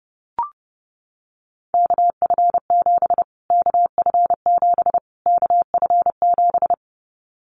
Morse code, heard as a keyed tone, about 30 wpm, 690 Hz. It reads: KF7 KF7 KF7
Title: Callsign Prefixes for Top 25 DXCC - Repeated 3x - 30wpm